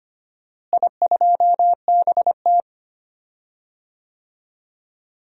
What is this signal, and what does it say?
Morse code, heard as a keyed tone, about 25 wpm, 700 Hz. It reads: I2BT
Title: DXCC-100 Call Signs - 4 Characters - Repeated 2x - 25wpm